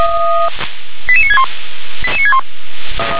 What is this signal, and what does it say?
Five-tone selective call (ZVEI), heard as quick runs of five tones, tones of about 70 ms. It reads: 88961 88961